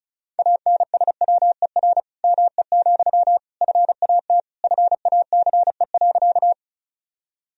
Morse code, heard as a keyed tone, about 36 wpm, 700 Hz. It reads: ANSWER ME, FAT FACE.
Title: Sentences from Top 1000 Words - Review Part 4 - 35wpm